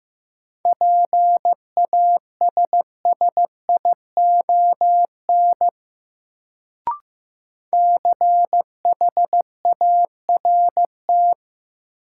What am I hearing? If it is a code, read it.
Morse code, heard as a keyed tone, about 15 wpm, 700 Hz. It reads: PASSION CHART
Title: Sentences from Top 1300 Words - New Word Drill Alt - 15wpm